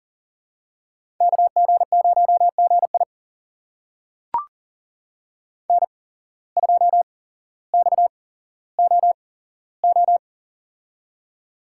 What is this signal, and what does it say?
Morse code, heard as a keyed tone, about 37 wpm, spaced as for 17 wpm, 700 Hz. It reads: KG0GI N2XOO
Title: USA Call Signs - 5 Characters - 8x Character Spacing - 40wpm